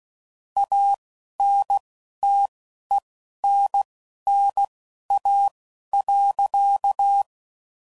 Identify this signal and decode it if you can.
Morse code, heard as a keyed tone, about 16 wpm, spaced as for 11 wpm, 800 Hz. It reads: ANTENNA.
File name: CW Code Groups @10WPM / QSOs / QSO kc7sac